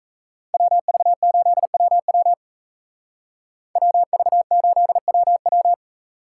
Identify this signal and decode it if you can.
Morse code, heard as a keyed tone, about 34 wpm, 700 Hz. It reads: WV8WW WV8WW
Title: Speed Racing - USA 2x2 Call Signs 28wpm